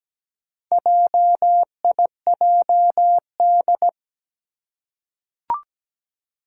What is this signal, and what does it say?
Morse code, heard as a keyed tone, about 17 wpm, 700 Hz. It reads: JIJD
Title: Four Challenging Letter-Number Combinations 17wpm